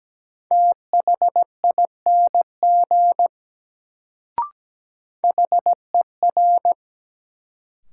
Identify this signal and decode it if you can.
Morse code, heard as a keyed tone, about 17 wpm, 700 Hz. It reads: THING HER
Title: Sentences from Top 100 Words - New Word Drill - 17wpm